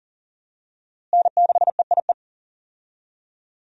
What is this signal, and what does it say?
Morse code, heard as a keyed tone, about 40 wpm, 700 Hz. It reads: N6EIE
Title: USA Call Signs - 5 Characters - Repeated 3x - 40wpm